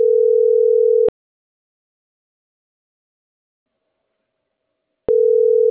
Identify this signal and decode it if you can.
Ringback tone (North America), 440 + 480 Hz.